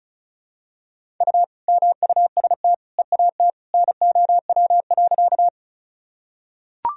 Morse code, heard as a keyed tone, about 34 wpm, 700 Hz. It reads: U MUST EAT NOW.
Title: Sentences from Top 300 Words and Abbreviations 35wpm